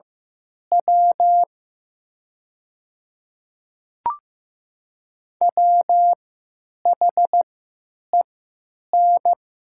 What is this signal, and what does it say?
Morse code, heard as a keyed tone, about 15 wpm, spaced as for 9 wpm, 700 Hz. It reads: W WHEN
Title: 3x Character Spacing - Top 100 Words - 15wpm